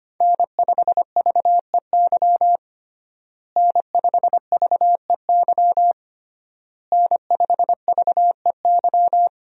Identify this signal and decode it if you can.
Morse code, heard as a keyed tone, about 25 wpm, 700 Hz. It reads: N5VEY N5VEY N5VEY
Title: USA Call Signs - 5 Characters - Repeated 3x - 25wpm